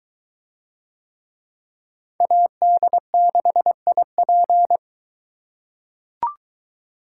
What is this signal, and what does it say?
Morse code, heard as a keyed tone, about 22 wpm, 700 Hz. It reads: AD6IP